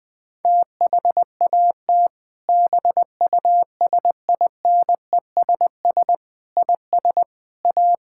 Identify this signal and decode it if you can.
Morse code, heard as a keyed tone, about 21 wpm, 700 Hz. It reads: THAT BUSINESS IS A
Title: Sentences from Top 1600 Words 20wpm